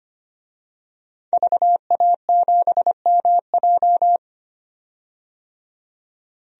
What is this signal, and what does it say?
Morse code, heard as a keyed tone, about 25 wpm, 700 Hz. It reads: VA7MJ
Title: DXCC-100 Call Signs - 5 Characters - 25wpm